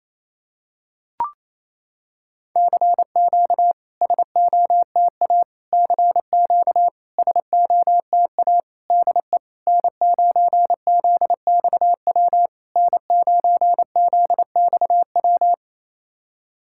Morse code, heard as a keyed tone, about 28 wpm, 700 Hz. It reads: CQ SOTA CQ SOTA DE N9ZXW N9ZXW
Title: SOTA QSO Practice 28wpm